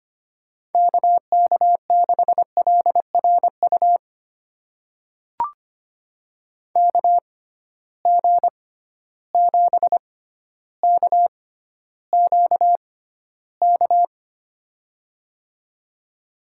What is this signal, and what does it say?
Morse code, heard as a keyed tone, about 25 wpm, spaced as for 13 wpm, 700 Hz. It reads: KK6LRU KG7KQK
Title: USA Call Signs - 6 Characters - 6x Character Spacing - 25wpm